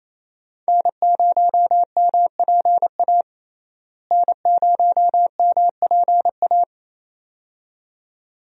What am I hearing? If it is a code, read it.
Morse code, heard as a keyed tone, about 28 wpm, 700 Hz. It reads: N0MPA N0MPA
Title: USA Call Signs - 5 Characters - Repeated 2x - 28wpm